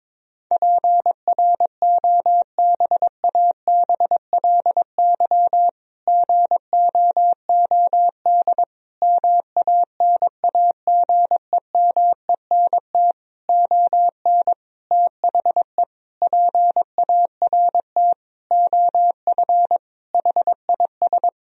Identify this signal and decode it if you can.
Morse code, heard as a keyed tone, about 22 wpm, 700 Hz. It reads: PROBABLY GOOD MANAGEMENT ON THE PART OF HIS
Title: Sentences from Top 1000 Words - Review Part 4 - 22wpm